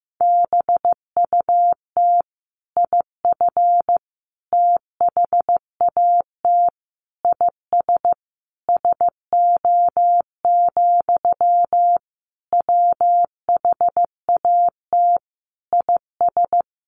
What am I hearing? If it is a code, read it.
Morse code, heard as a keyed tone, about 15 wpm, 700 Hz. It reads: BUT IF THAT IS SO, WHAT IS
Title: Sentences from Top 200 Words 15wpm